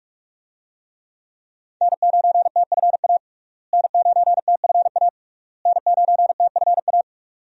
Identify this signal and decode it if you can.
Morse code, heard as a keyed tone, about 44 wpm, 700 Hz. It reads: N9TFA N9TFA N9TFA
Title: USA Call Signs - 5 Characters - Repeated 3x - 45wpm